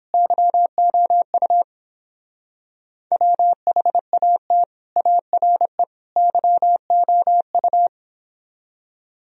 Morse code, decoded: YOU WHAT ARE YOU

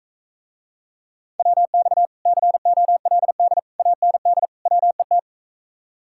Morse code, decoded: WX COLD AND WET

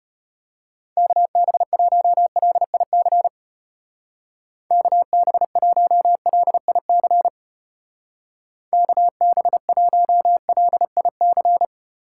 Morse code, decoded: KB1LIC KB1LIC KB1LIC

